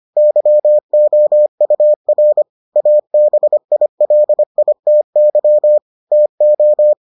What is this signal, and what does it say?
Morse code, heard as a keyed tone, about 25 wpm, 600 Hz. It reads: YOUR ABILITY TO